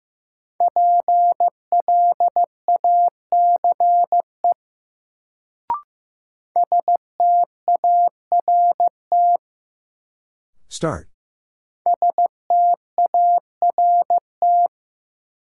Morse code, heard as a keyed tone, about 15 wpm, 700 Hz. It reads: PLACE START START
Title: Sets of 4 Words - New Word Drill - 15wpm